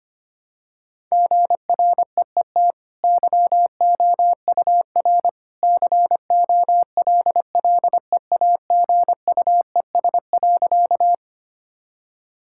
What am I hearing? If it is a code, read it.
Morse code, heard as a keyed tone, about 25 wpm, 700 Hz. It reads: GREET YOUR COLLEAGUES.
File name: Easy Sentences with Hints - 25wpm